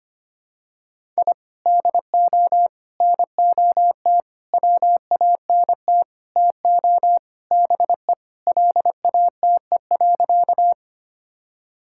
Morse code, heard as a keyed tone, about 25 wpm, 700 Hz. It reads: I DO NOT WANT TO BE LATE.
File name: practice / 25wpm / Sentences with 4-Word Phrases - 25wpm